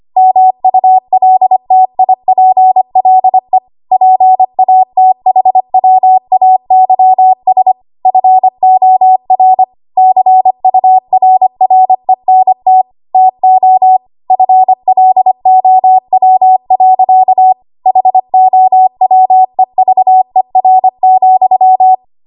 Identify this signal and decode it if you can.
Morse code, heard as a keyed tone, about 25 wpm, 750 Hz. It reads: MULTIPLE PATHWAYS FOR CURRENT TO FLOW. HOWEVER,
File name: Archive / 25 WPM / 25WPM